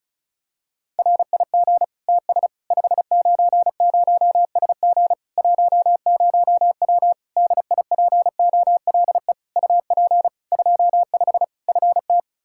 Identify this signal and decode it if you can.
Morse code, heard as a keyed tone, about 34 wpm, 700 Hz. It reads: RIG TS 590SG 10W DIPOLE UP 25 FT
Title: K1USN SST QSO Practice 35wpm